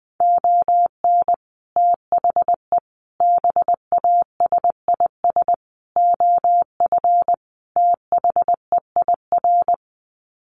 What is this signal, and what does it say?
Morse code, heard as a keyed tone, about 20 wpm, 700 Hz. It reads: ON THE BASIS OF THEIR
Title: Sets of 5 Words Encore 20wpm